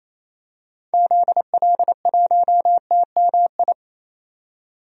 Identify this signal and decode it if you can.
Morse code, heard as a keyed tone, about 28 wpm, 700 Hz. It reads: ZL1TMI
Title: DXCC-100 Call Signs - 6 Characters - 28wpm